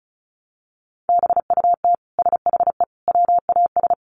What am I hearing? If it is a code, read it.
Morse code, heard as a keyed tone, about 34 wpm, 700 Hz. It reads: BUT SHE WAS
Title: Sentences from Top 700 Words 35wpm